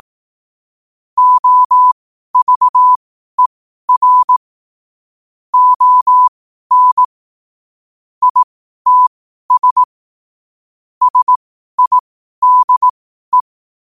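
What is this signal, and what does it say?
Morse code, heard as a keyed tone, about 18 wpm, spaced as for 12 wpm, 1000 Hz. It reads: OVER ON ITS SIDE
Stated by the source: Morse at 12 wpm.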